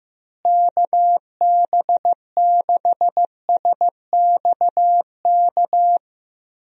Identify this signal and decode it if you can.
Morse code, heard as a keyed tone, about 15 wpm, 700 Hz. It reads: KB6SXK
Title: USA Call Signs - 6 Characters - Repeated 2x - 15wpm